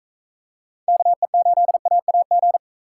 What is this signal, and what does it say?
Morse code, heard as a keyed tone, about 44 wpm, 700 Hz. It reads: KE8AAG